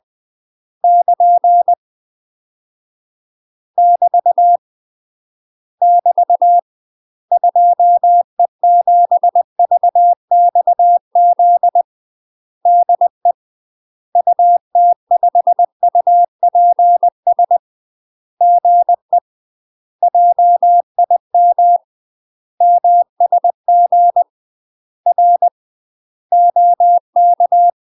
Morse code, heard as a keyed tone, about 20 wpm, 700 Hz. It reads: <KN> <BT> <BT> 2E7VXZ DE UT5UPS GE JIM MSG R OK
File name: UK MASTER QSOs TRAINING / 20WPM / UK MASTER QSOs 0002